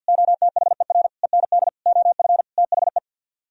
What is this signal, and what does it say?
Morse code, heard as a keyed tone, about 61 wpm, 700 Hz. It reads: OTHER END OF THE
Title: Speed-Racing - Sets of 5 Words 50wpm